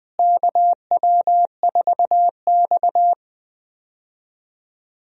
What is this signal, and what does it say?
Morse code, heard as a keyed tone, about 20 wpm, 700 Hz. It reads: KW4X
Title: USA Call Signs - 4 Characters - Repeated 2x - 20wpm